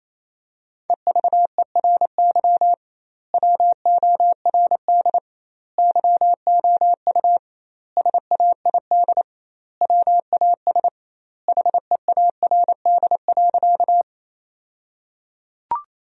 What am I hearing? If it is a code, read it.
Morse code, heard as a keyed tone, about 27 wpm, 700 Hz. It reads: EVERY WORD YOU SAID WAS HEARD.
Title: 2x Word-Spacing - Sentences from Top 400 Words - 28wpm